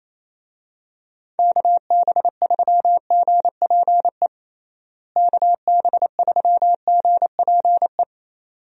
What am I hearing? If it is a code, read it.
Morse code, heard as a keyed tone, about 27 wpm, 700 Hz. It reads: KB3GPE KB3GPE
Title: USA Call Signs - 6 Characters - Repeated 2x - 28wpm